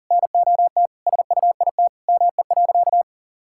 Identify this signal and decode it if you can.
Morse code, heard as a keyed tone, about 40 wpm, 700 Hz. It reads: NOT SUIT ME.
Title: Sentences from Top 1600 Words 40wpm